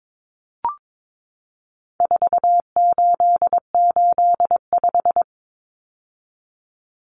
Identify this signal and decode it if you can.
Morse code, heard as a keyed tone, about 22 wpm, 700 Hz. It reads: 4885
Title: Common Cut Numbers - 4-Digit 22wpm